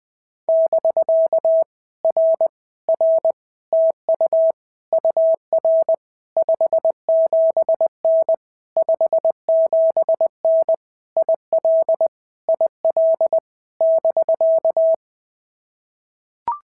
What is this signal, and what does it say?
Morse code, heard as a keyed tone, about 19 wpm, 650 Hz. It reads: <BK> R R TU UR 57N 57N IL IL <BK>
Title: POTA QSO Practice - 20wpm